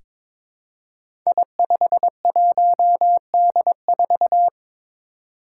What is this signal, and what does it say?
Morse code, heard as a keyed tone, about 22 wpm, 700 Hz. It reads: I51D4
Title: Code Groups - 5 Letter:Numbers - 22wpm